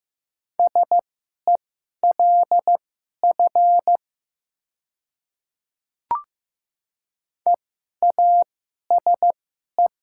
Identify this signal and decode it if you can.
Morse code, heard as a keyed tone, about 15 wpm, spaced as for 11 wpm, 700 Hz. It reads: SELF EASE
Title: 2x Character Spacing - Top 300 Words - 15wpm